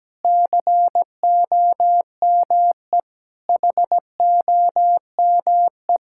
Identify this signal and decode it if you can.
Morse code, heard as a keyed tone, about 17 wpm, 700 Hz. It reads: COME HOME
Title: Sentences from Top 1000 Words - Review Part 5 - 17wpm